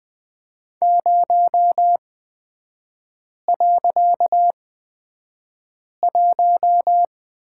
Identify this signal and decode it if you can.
Morse code, heard as a keyed tone, about 20 wpm, spaced as for 5 wpm, 700 Hz. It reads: 0.1
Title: 255 Lesson 41e - Period - Three Characters 20wpm